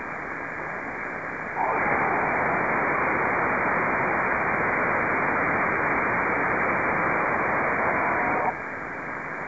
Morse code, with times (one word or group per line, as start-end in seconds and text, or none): none